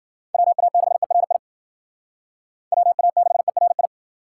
Morse code, heard as a keyed tone, about 52 wpm, 700 Hz. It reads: WA6ERI WA6ERI